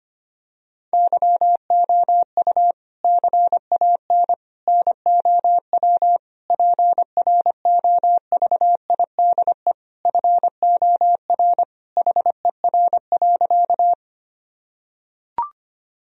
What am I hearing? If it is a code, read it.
Morse code, heard as a keyed tone, about 25 wpm, 700 Hz. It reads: YOU CAN NOW PROVIDE FOR HER.
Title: Sentences from Top 1000 Words - Review Part 3 - 25wpm